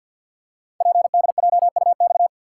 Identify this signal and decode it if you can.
Morse code, heard as a keyed tone, about 48 wpm, 700 Hz. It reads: PDJUX
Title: Code Groups - 5 Letters - 50wpm